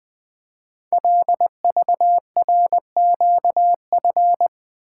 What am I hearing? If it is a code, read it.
Morse code, heard as a keyed tone, about 20 wpm, 700 Hz. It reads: LVRQF
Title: Code Groups - 5 Letters - 20wpm